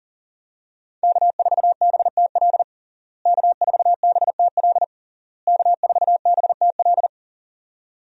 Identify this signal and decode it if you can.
Morse code, heard as a keyed tone, about 37 wpm, 700 Hz. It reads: K4BTL K4BTL K4BTL
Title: USA Call Signs - 5 Characters - Repeated 3x - 40wpm